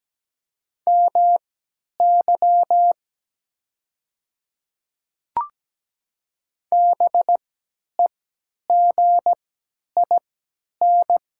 Morse code, heard as a keyed tone, about 17 wpm, spaced as for 10 wpm, 700 Hz. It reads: MY BEGIN
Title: 3x Character Spacing - Top 300 Words - 17wpm